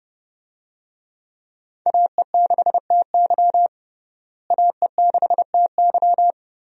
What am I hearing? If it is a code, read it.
Morse code, heard as a keyed tone, about 30 wpm, 700 Hz. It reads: AE6TY AE6TY
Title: USA Call Signs - 5 Characters - Repeated 2x - 30wpm